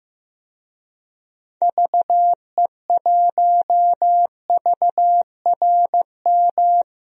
Morse code, heard as a keyed tone, about 15 wpm, 700 Hz. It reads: VE1VRM
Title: DXCC-100 Call Signs - 6 Characters - 15wpm